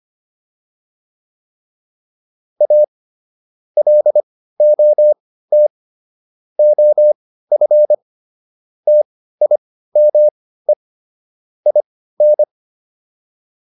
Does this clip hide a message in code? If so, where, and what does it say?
Yes: Morse code from 2.6 to 12.4 s, heard as a keyed tone, about 25 wpm, spaced as for 15 wpm, 600 Hz. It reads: A LOT OF TIME IN